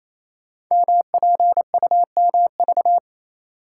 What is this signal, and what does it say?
Morse code, heard as a keyed tone, about 27 wpm, 690 Hz. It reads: MPUMV
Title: Code Groups - 5 Letters - 28wpm